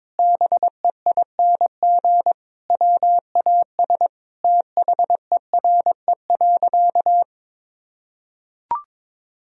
Morse code, heard as a keyed tone, about 22 wpm, 700 Hz. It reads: BEING WAS THERE.
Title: Sentences from Top 1000 Words - Review Part 4 - 22wpm